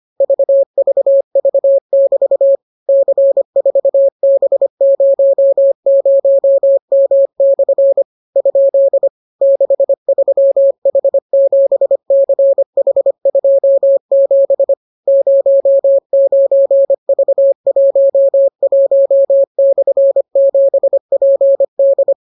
Morse code, heard as a keyed tone, about 25 wpm, 550 Hz. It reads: VVV<BT> C4B00M/ ? 63H7CH27 09V11/7PD